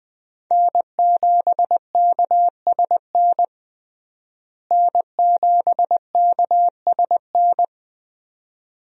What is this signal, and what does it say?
Morse code, heard as a keyed tone, about 21 wpm, 700 Hz. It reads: N7KSN N7KSN
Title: USA Call Signs - 5 Characters - Repeated 2x - 20wpm